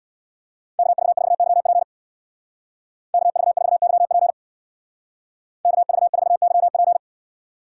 Morse code, heard as a keyed tone, about 58 wpm, 700 Hz. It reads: KV4YP KV4YP KV4YP
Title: Speed Racing - USA Call Signs 50wpm